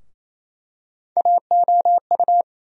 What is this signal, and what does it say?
Morse code, heard as a keyed tone, about 28 wpm, 700 Hz. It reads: AOU
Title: USA Call Sign Postfixes - 3 Letters 28wpm